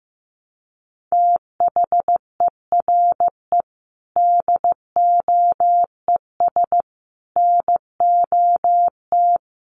Morse code, heard as a keyed tone, about 15 wpm, 700 Hz. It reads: THERE DOES NOT
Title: Sentences from Top 1400 Words 15wpm